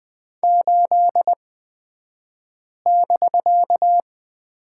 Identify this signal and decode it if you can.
Morse code, heard as a keyed tone, about 20 wpm, spaced as for 5 wpm, 700 Hz. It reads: 8<BK>
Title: 249 Lesson 40d - BK - Two Characters 20wpm